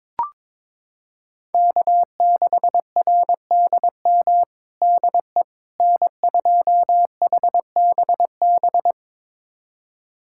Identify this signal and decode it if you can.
Morse code, heard as a keyed tone, about 22 wpm, 700 Hz. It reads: K6RDM DE N2HBB
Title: USA Callsign DE Callsign - 5 Characters - 22wpm